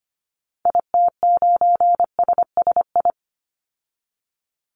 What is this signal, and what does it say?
Morse code, heard as a keyed tone, about 25 wpm, 700 Hz. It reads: IT9SSI
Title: Call Signs Repeated 3x - 25wpm